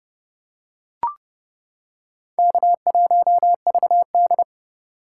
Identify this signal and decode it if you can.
Morse code, heard as a keyed tone, about 29 wpm, 700 Hz. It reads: K1VD